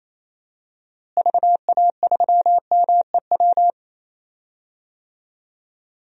Morse code, heard as a keyed tone, about 28 wpm, 700 Hz. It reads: VA3MEW
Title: DXCC-100 Call Signs - 6 Characters - Repeated 3x - 28wpm